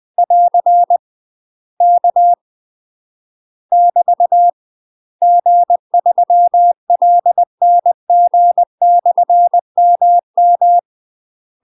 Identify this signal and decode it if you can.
Morse code, heard as a keyed tone, about 20 wpm, 700 Hz. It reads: <AR> K <BT> G3LNG/MM
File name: UK MASTER QSOs TRAINING / 20WPM / UK MASTER QSOs 0031